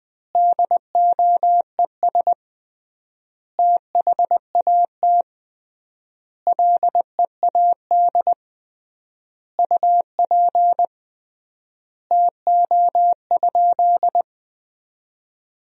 Morse code, heard as a keyed tone, about 20 wpm, 700 Hz. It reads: DOES THAT LEAD UP TO?
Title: 3x Word-Spacing - Sentences from Top 500 Words - 20wpm